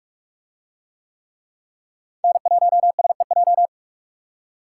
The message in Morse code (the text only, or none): N1SEJ